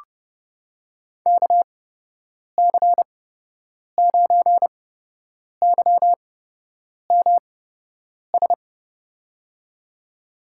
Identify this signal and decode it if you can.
Morse code, heard as a keyed tone, about 30 wpm, spaced as for 8 wpm, 700 Hz. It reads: KC9YMS
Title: USA Call Signs - 6 Characters - 8x Character Spacing - 30wpm